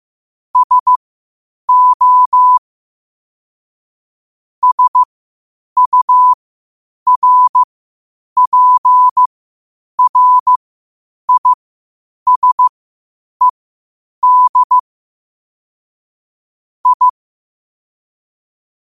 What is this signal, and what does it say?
Morse code, heard as a keyed tone, about 15 wpm, spaced as for 8 wpm, 1000 Hz. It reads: SO SURPRISED I
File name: qotdfiles / 8wpm